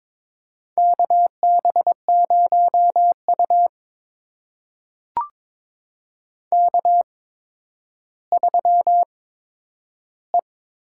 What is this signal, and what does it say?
Morse code, heard as a keyed tone, about 22 wpm, spaced as for 11 wpm, 700 Hz. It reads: KB0U K3E
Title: USA Call Signs - 8x Character Spacing - 22wpm